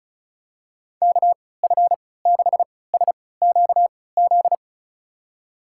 Morse code, heard as a keyed tone, about 36 wpm, spaced as for 20 wpm, 700 Hz. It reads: KF6SQZ
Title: USA Call Signs - 6 Characters - 3x Character Spacing - 35wpm